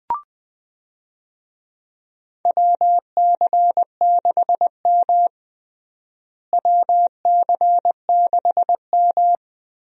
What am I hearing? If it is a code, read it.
Morse code, heard as a keyed tone, about 20 wpm, 700 Hz. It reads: WC6M WC6M
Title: USA Call Signs - 4 Characters - Repeated 2x - 20wpm